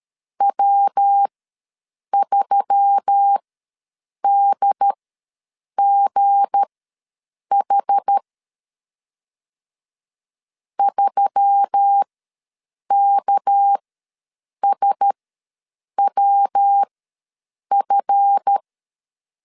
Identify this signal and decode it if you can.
Morse code, heard as a keyed tone, about 13 wpm, spaced as for 7 wpm, 800 Hz. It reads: W3DGH 3KSWF
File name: LEZIONI / 7 GF9Z / 7 GF ws 12wpm 3140